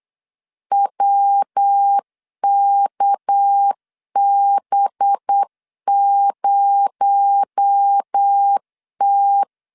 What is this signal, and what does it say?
Morse code, heard as a keyed tone, about 8 wpm, 800 Hz. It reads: WKB0T
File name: LEZIONI / 8 YJ45 / 8 Y 8wpm 3245